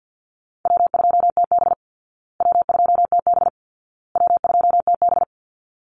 Morse code, heard as a keyed tone, about 50 wpm, 700 Hz. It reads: W2TB W2TB W2TB
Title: Easy Call Signs Repeated 3x - 50wpm